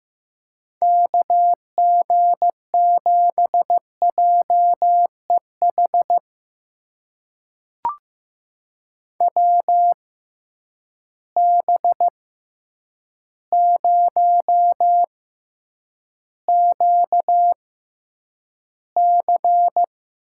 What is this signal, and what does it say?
Morse code, heard as a keyed tone, about 15 wpm, spaced as for 8 wpm, 700 Hz. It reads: KG7JEH WB0QC
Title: USA Call Signs - 6 Characters - 6x Character Spacing - 15wpm